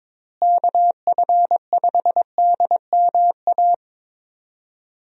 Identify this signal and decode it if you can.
Morse code, heard as a keyed tone, about 21 wpm, 700 Hz. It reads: KF5DMA
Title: USA Call Signs - 6 Characters - Repeated 3x - 22wpm